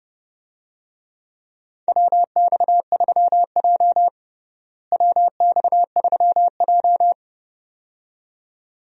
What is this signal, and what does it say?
Morse code, heard as a keyed tone, about 30 wpm, 700 Hz. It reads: WX3J WX3J
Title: USA Call Signs - 4 Characters - Repeated 2x - 30wpm